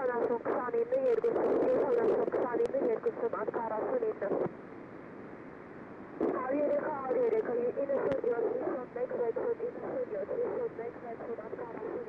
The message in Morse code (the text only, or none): none